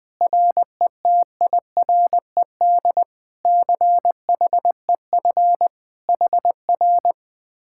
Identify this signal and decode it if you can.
Morse code, heard as a keyed tone, about 20 wpm, 700 Hz. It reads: RETIRED CHEF HR
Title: Speed-Racing Common QSO Phrases - 20wpm